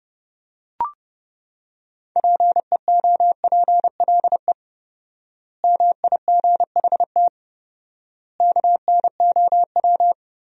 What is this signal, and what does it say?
Morse code, heard as a keyed tone, about 30 wpm, 700 Hz. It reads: PEOPLE MIGHT KNOW
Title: 4x Word-Spacing - Sentences from Top 500 Words - 30wpm